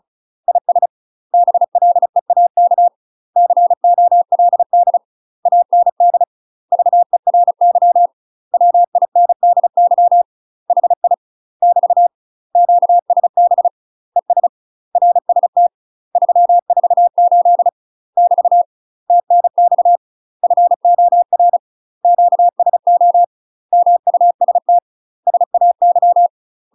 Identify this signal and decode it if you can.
Morse code, heard as a keyed tone, about 34 wpm, 700 Hz. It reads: IS BLEAK COLD AND VERY WINDY HI <BT> QSB ES RST 348 <BT> TNX FOR QSO MUST SAY
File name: UK MASTER QSOs TRAINING/35WPM